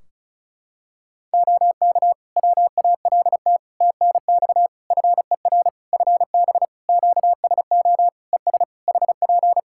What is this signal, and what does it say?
Morse code, heard as a keyed tone, about 35 wpm, 700 Hz. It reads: OK WALT TNX FER FB QSO ES HP
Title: Practice QSO - Exchange 4 - Ending - 35wpm